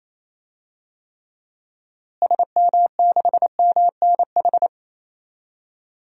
Morse code, heard as a keyed tone, about 28 wpm, 700 Hz. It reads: SM6MNH
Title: DXCC-100 Call Signs - 6 Characters - Repeated 2x - 28wpm